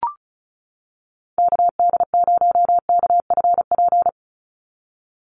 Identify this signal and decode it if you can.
Morse code, heard as a keyed tone, about 34 wpm, 700 Hz. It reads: KD0KFP
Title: USA Call Signs 35wpm